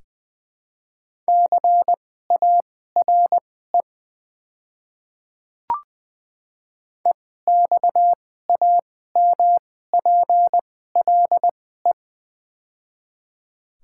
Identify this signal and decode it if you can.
Morse code, heard as a keyed tone, about 20 wpm, spaced as for 14 wpm, 700 Hz. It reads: CARE EXAMPLE
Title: 2x Character Spacing - Top 300 Words - 20wpm